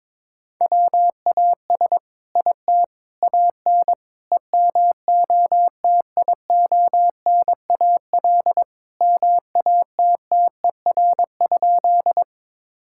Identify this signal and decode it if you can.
Morse code, heard as a keyed tone, about 22 wpm, 700 Hz. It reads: WAS IT AN EMOTIONAL MATTER?
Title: Sentences from Top 1000 Words - Review Part 5 - 22wpm